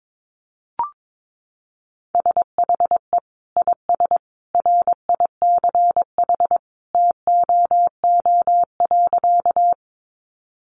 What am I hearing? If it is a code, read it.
Morse code, heard as a keyed tone, about 22 wpm, 700 Hz. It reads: SHE IS RICH TOO.
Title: Sentences from Top 1100 Words 22wpm